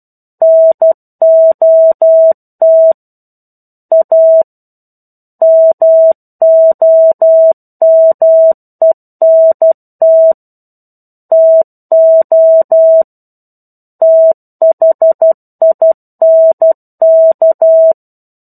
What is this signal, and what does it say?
Morse code, heard as a keyed tone, about 12 wpm, 650 Hz. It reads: NOT A MOMENT TO THINK
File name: 01-Chapter-01 12 WPM